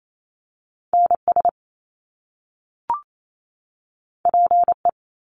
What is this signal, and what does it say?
Morse code, heard as a keyed tone, about 28 wpm, 700 Hz. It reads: NS PE
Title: Canadian Provinces and Territories 28wpm